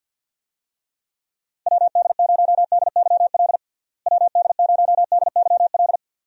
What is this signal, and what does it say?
Morse code, heard as a keyed tone, about 52 wpm, 700 Hz. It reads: WD0DYL WD0DYL
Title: USA Call Signs - 6 Characters - Repeated 2x - 50wpm